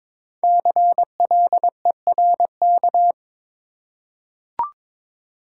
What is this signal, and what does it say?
Morse code, heard as a keyed tone, about 22 wpm, 700 Hz. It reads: CLERK
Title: Five Letter Words - 22wpm